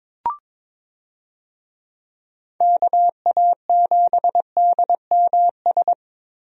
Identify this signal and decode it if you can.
Morse code, heard as a keyed tone, about 22 wpm, 700 Hz. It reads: KA7DMS
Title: USA Call Signs - 6 Characters - 22wpm